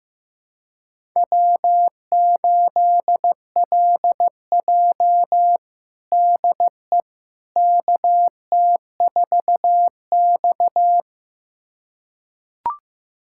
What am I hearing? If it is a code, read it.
Morse code, heard as a keyed tone, about 15 wpm, 700 Hz. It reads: W8LJ DE KT4X
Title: USA Callsign DE Callsign - 4 Characters - 15wpm